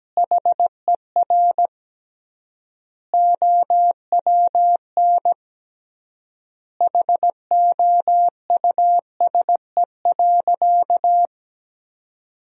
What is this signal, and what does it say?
Morse code, heard as a keyed tone, about 17 wpm, 700 Hz. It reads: HER OWN HOUSE.
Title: 3x Word-Spacing - Sentences from Top 300 Words - 17wpm